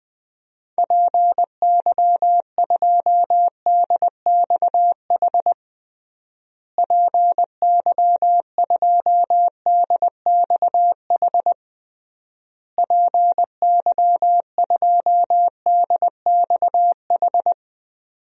Morse code, heard as a keyed tone, about 20 wpm, 700 Hz. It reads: PY2DXH PY2DXH PY2DXH
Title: DXCC-100 Call Signs - 6 Characters - Repeated 3x - 20wpm